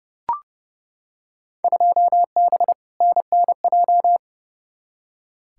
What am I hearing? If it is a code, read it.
Morse code, heard as a keyed tone, about 30 wpm, 700 Hz. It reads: 2B NNJ
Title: Field Day Exchanges Encore - 30wpm